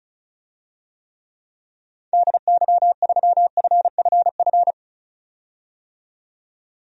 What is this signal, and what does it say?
Morse code, heard as a keyed tone, about 36 wpm, 700 Hz. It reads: DY3FFF
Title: DXCC-100 Call Signs - 6 Characters - 35wpm